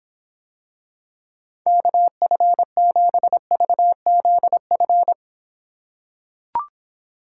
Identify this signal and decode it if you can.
Morse code, heard as a keyed tone, about 24 wpm, 700 Hz. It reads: KF7VZF